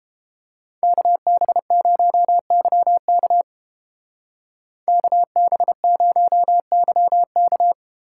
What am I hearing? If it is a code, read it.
Morse code, heard as a keyed tone, about 30 wpm, 700 Hz. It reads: KB0YK KB0YK